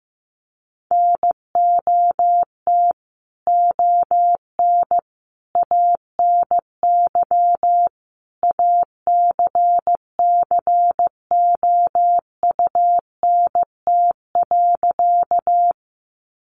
Morse code, heard as a keyed tone, about 15 wpm, 700 Hz. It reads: NOT ON ANY ACCOUNT.
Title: Sentences from Top 900 Words 15wpm